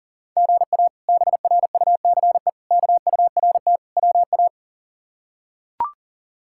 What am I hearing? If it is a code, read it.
Morse code, heard as a keyed tone, about 40 wpm, 700 Hz. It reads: GA BRUCE KURT WA
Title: K1USN SST QSO Practice - 40wpm